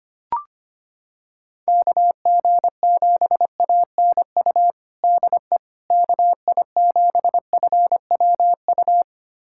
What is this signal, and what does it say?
Morse code, heard as a keyed tone, about 25 wpm, 700 Hz. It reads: KG7ANU DE KI7FWU
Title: USA Callsign DE Callsign - 6 Characters - 25wpm